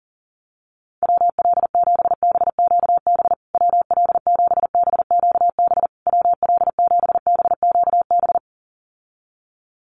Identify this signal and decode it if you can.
Morse code, heard as a keyed tone, about 40 wpm, 700 Hz. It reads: WL7BQB WL7BQB WL7BQB
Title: USA Call Signs Repeated 3x 40wpm